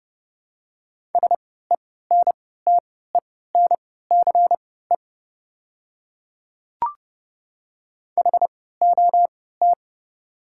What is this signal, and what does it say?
Morse code, heard as a keyed tone, about 30 wpm, spaced as for 17 wpm, 700 Hz. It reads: SENTENCE HOT
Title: 3x Character Spacing - Top 200 Words - 30wpm